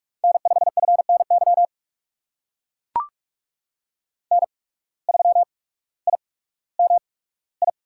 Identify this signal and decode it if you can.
Morse code, heard as a keyed tone, about 44 wpm, spaced as for 19 wpm, 700 Hz. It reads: N5FNY N3IMI
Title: USA Call Signs - 5 Characters - 8x Character Spacing - 45wpm